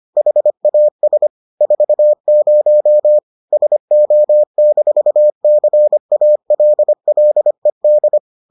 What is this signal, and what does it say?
Morse code, heard as a keyed tone, about 25 wpm, 600 Hz. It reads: HAS 40 SO-CALLED